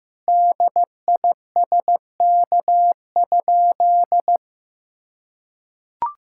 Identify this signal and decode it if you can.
Morse code, heard as a keyed tone, about 15 wpm, 700 Hz. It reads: DISK?
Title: Sentences from Top 1000 Words - Review Part 6 - 15wpm